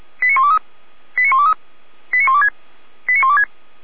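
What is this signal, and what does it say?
Five-tone selective call (ZVEI), heard as quick runs of five tones, tones of about 70 ms. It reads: 89123 89123 89126 89126
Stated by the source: ZVEI five-tone selective call 89770 received by SDR.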